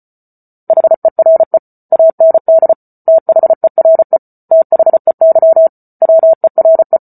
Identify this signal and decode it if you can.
Morse code, heard as a keyed tone, about 34 wpm, 650 Hz. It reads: HERE AND THERE THEY WERE